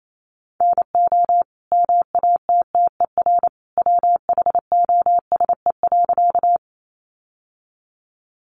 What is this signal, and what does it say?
Morse code, heard as a keyed tone, about 28 wpm, 700 Hz. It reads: NO MATTER WHOSE.
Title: Sentences from Top 1200 Words 28wpm